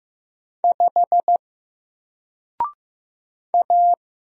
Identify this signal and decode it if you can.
Morse code, heard as a keyed tone, about 15 wpm, 700 Hz. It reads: 5 A
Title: Single Letter-Number Alt 15wpm